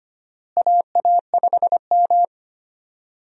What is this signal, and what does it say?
Morse code, heard as a keyed tone, about 25 wpm, 700 Hz. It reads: AA5M
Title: USA Call Signs - 4 Characters - Repeated 2x - 25wpm